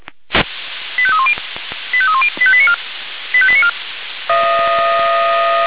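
Five-tone selective call (ZVEI), heard as quick runs of five tones, tones of about 70 ms. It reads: 85310 85310 85894 85894
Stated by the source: ZVEI five-tone call 85259 received via SDR.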